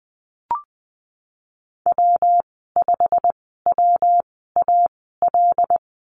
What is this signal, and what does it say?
Morse code, heard as a keyed tone, about 20 wpm, spaced as for 15 wpm, 700 Hz. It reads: W5WAL
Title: 186 Lesson 30g - Letter b - Call Signs 20wpm